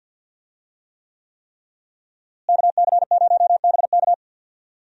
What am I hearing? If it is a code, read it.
Morse code, heard as a keyed tone, about 50 wpm, 700 Hz. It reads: KC0BK
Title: USA Call Signs - 5 Characters - Repeated 3x - 50wpm